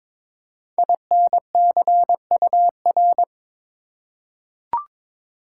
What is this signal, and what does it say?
Morse code, heard as a keyed tone, about 22 wpm, 700 Hz. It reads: INCUR
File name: 5 Letter Words - First 3 Letters Given - 22wpm